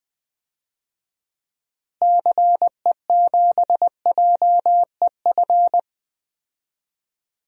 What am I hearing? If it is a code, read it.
Morse code, heard as a keyed tone, about 21 wpm, 700 Hz. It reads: CE7JEF
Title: DXCC-100 Call Signs - 6 Characters - Repeated 2x - 20wpm